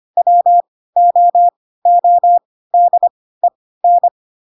Morse code, heard as a keyed tone, about 25 wpm, spaced as for 16 wpm, 700 Hz. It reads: WOODEN
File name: Pinocchio / Pinocchio ch3 10 25 ik1hge